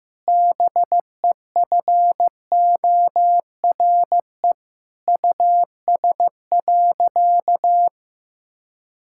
Morse code, heard as a keyed tone, about 15 wpm, 700 Hz. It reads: BEFORE US.
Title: Sentences from Top 1000 Words - Review Part 5 - 15wpm